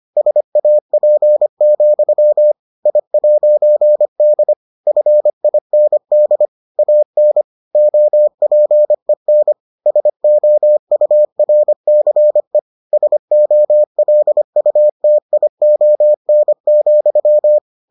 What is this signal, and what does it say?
Morse code, heard as a keyed tone, about 25 wpm, 600 Hz. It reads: SAP, I'D FIND AN OPEN SOURCE SOLUTION,